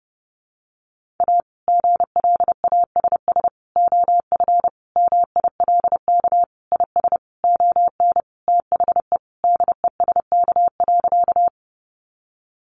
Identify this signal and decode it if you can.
Morse code, heard as a keyed tone, about 30 wpm, 700 Hz. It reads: A GLASS OF MILK IS ON THE DESK.
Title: Additional Easy Phrases 30wpm